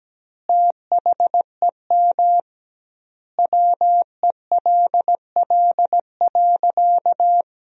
Morse code, heard as a keyed tone, about 17 wpm, 700 Hz. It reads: THEM WELL.
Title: 2x Word-Spacing - Sentences from Top 300 Words - 17wpm